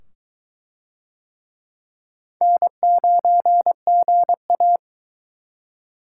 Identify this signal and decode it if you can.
Morse code, heard as a keyed tone, about 22 wpm, 700 Hz. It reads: N9GA